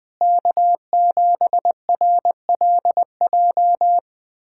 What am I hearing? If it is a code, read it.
Morse code, heard as a keyed tone, about 20 wpm, 700 Hz. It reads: K7RLJ
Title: USA Callsign DE Callsign - 5 Characters - 20wpm